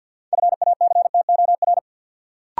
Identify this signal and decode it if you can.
Morse code, heard as a keyed tone, about 52 wpm, 700 Hz. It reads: FACTOR
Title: Sentences from Top 1300 Words - New Word Drill Alt - 50wpm